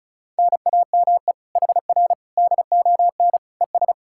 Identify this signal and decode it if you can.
Morse code, heard as a keyed tone, about 37 wpm, 700 Hz. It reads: NAME HR DON ES
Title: Speed-Racing Common QSO Phrases Encore - 30wpm